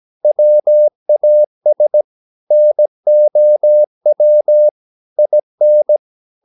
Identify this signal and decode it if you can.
Morse code, heard as a keyed tone, about 17 wpm, 600 Hz. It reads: WAS NOW IN
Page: Romanen Never af Ken Follet på engelsk.